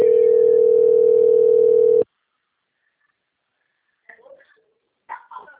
Ringback tone (North America), 440 + 480 Hz.